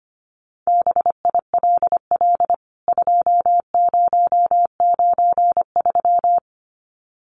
Morse code, heard as a keyed tone, about 25 wpm, 700 Hz. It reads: BILL 2093
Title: CWT Exchange 25wpm